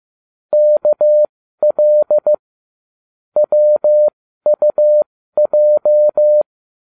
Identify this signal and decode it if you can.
Morse code, heard as a keyed tone, about 15 wpm, spaced as for 12 wpm, 600 Hz. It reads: KL WUJ